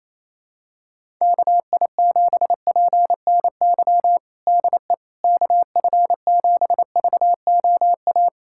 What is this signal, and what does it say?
Morse code, heard as a keyed tone, about 28 wpm, 700 Hz. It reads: KI7PNY DE KF7VOA
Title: USA Callsign DE Callsign - 6 Characters - 28wpm